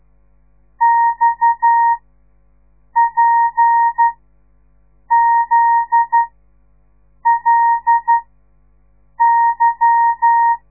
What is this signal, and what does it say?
Morse code, heard as a keyed tone, about 12 wpm, spaced as for 6 wpm, 930 Hz. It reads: XPZLY
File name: radiomorningstar / morse / les 8